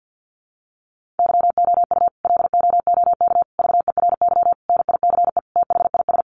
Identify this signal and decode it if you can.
Morse code, heard as a keyed tone, about 50 wpm, 700 Hz. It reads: YOU LOOK VERY NICE THIS
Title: Sentences from Top 1500 Words 50wpm